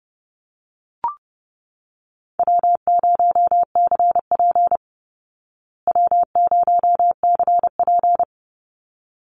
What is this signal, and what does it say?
Morse code, heard as a keyed tone, about 30 wpm, 700 Hz. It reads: W0CP W0CP
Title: Easy Call Signs Repeated 2x - 30wpm